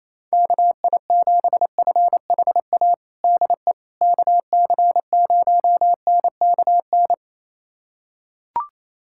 Morse code, heard as a keyed tone, about 28 wpm, 700 Hz. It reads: KI7FHA DE KC0NKN
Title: USA Callsign DE Callsign 28wpm